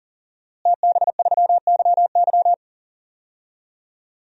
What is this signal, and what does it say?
Morse code, heard as a keyed tone, about 40 wpm, 700 Hz. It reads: TB3YY
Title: DXCC-100 Call Signs - 5 Characters - Repeated 2x - 40wpm